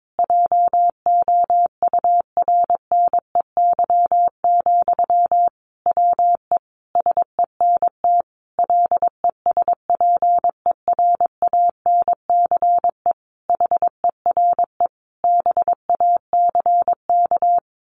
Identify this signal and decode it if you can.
Morse code, heard as a keyed tone, about 22 wpm, 700 Hz. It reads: JOURNEY, WE SENT LESPERANCE HERE BACK
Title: A Sound Of Thunder 22wpm